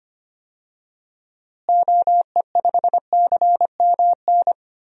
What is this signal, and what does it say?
Morse code, heard as a keyed tone, about 25 wpm, 700 Hz. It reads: OE5CMN
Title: DXCC-100 Call Signs - 6 Characters - Repeated 2x - 25wpm